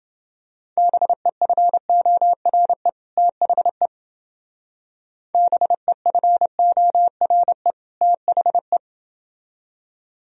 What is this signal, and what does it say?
Morse code, heard as a keyed tone, about 30 wpm, 700 Hz. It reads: BEFORE THE BEFORE THE